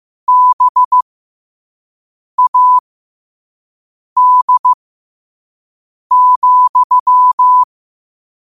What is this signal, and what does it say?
Morse code, heard as a keyed tone, about 15 wpm, spaced as for 5 wpm, 1000 Hz. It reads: BAD,
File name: qotdfiles / 5wpm